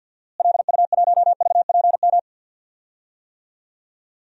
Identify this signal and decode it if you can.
Morse code, heard as a keyed tone, about 53 wpm, 700 Hz. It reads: LU1VPM
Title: DXCC-100 Call Signs - 6 Characters - Repeated 2x - 50wpm